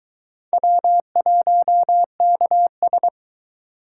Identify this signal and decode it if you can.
Morse code, heard as a keyed tone, about 22 wpm, 700 Hz. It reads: W1KS